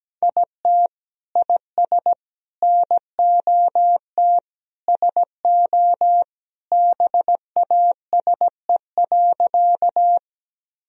Morse code, heard as a keyed tone, about 17 wpm, 700 Hz. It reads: IT IS NOT SO BASE.
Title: Sentences from Top 1000 Words - Review Part 1 - 17wpm